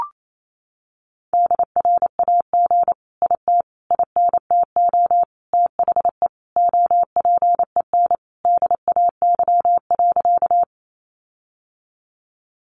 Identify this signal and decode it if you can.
Morse code, heard as a keyed tone, about 28 wpm, 700 Hz. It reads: DRAG IT INTO THE OPEN DAY.
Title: Sentences from Top 1500 Words 28wpm